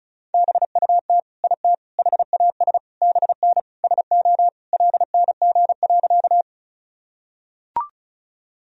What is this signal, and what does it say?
Morse code, heard as a keyed tone, about 36 wpm, 700 Hz. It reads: BUT IT HAS BN SO LNG.
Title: Sentences from Top 100 Words and Abbreviations 35wpm